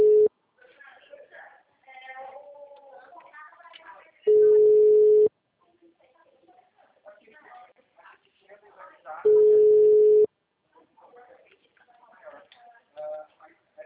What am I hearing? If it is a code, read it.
Ringback tone (Europe, 425 Hz), with speech.
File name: out- -2017- - - .506